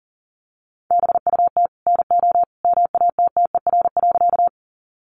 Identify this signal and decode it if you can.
Morse code, heard as a keyed tone, about 38 wpm, 700 Hz. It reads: BUT NO MATTER.
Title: Sentences from Top 700 Words 40wpm